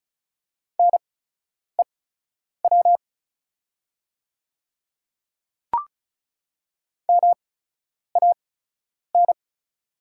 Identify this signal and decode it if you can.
Morse code, heard as a keyed tone, about 36 wpm, spaced as for 10 wpm, 690 Hz. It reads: NEW MAN
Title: 8x Character Spacing - Top 300 Words - 35wpm